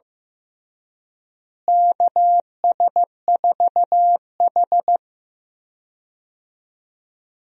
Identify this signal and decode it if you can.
Morse code, heard as a keyed tone, about 15 wpm, 700 Hz. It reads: KS4H
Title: USA Call Signs - 4 Characters - Repeated 3x - 15wpm